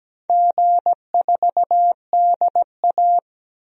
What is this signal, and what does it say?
Morse code, heard as a keyed tone, about 17 wpm, 700 Hz. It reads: G4DA
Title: DXCC-100 Call Signs - 4 Characters - 17wpm